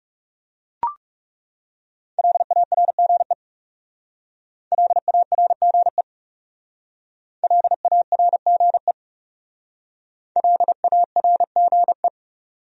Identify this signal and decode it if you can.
Morse code, heard as a keyed tone, about 38 wpm, 700 Hz. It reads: LARGE LARGE LARGE LARGE